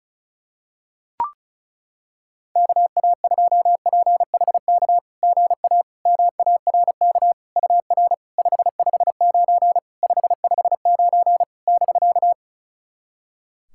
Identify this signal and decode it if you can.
Morse code, heard as a keyed tone, about 36 wpm, 690 Hz. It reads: KA2PHK GA MARK UR 559 559 <BK>
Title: SOTA QSO Practice 35wpm